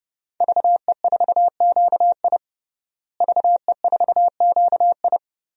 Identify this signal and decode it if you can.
Morse code, heard as a keyed tone, about 30 wpm, 700 Hz. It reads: VE4QI VE4QI
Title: DXCC-100 Call Signs - 5 Characters - Repeated 3x - 30wpm